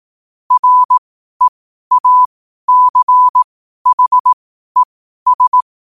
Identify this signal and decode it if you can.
Morse code, heard as a keyed tone, about 18 wpm, spaced as for 12 wpm, 1000 Hz. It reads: REACHES
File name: qotdfiles / 12wpm